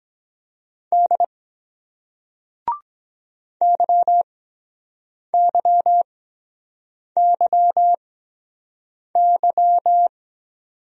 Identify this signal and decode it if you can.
Morse code, heard as a keyed tone, about 23 wpm, spaced as for 7 wpm, 700 Hz. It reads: D YYYY